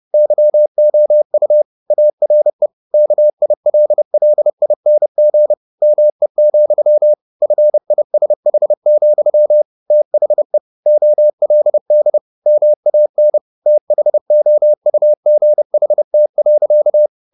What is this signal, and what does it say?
Morse code, heard as a keyed tone, about 30 wpm, 600 Hz. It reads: YOU ARE KILLING ME, FISH, THE OLD MAN THOUGHT.